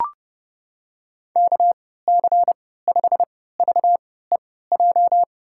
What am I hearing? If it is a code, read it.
Morse code, heard as a keyed tone, about 32 wpm, spaced as for 17 wpm, 700 Hz. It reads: KC5VEJ
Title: USA Call Signs - 6 Characters - 3x Character Spacing - 30wpm